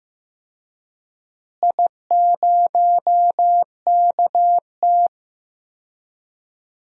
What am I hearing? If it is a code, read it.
Morse code, heard as a keyed tone, about 15 wpm, 700 Hz. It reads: I0KT
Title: DXCC-100 Call Signs - 4 Characters - Repeated 3x - 15wpm